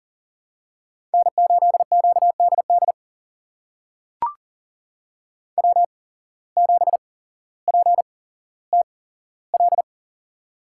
Morse code, heard as a keyed tone, about 42 wpm, spaced as for 17 wpm, 700 Hz. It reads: N8QDD W7PTL
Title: USA Call Signs - 8x Character Spacing - 40wpm